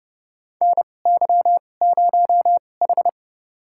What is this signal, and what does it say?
Morse code, heard as a keyed tone, about 31 wpm, spaced as for 22 wpm, 700 Hz. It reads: NY0H
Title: USA Call Signs - 4 Characters - 2x Character Spacing - 30wpm